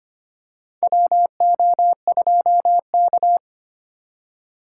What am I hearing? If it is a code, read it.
Morse code, heard as a keyed tone, about 25 wpm, 700 Hz. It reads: WO2K
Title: USA Call Signs - 4 Characters - 3x Character Spacing - 25wpm